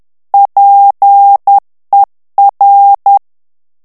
Morse code, heard as a keyed tone, about 10 wpm, 800 Hz. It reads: PER
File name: morse mp3 / ITALIEN texte clair 1 vitesse 10 mots minute